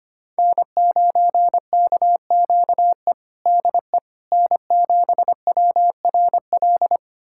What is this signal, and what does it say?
Morse code, heard as a keyed tone, about 25 wpm, 700 Hz. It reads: N9KQE DE N7WRL
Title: USA Callsign DE Callsign - 5 Characters - 25wpm